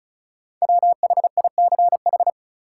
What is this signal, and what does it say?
Morse code, heard as a keyed tone, about 34 wpm, 700 Hz. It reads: WHICH
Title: Sentences from Top 100 Words - New Word Drill - 35wpm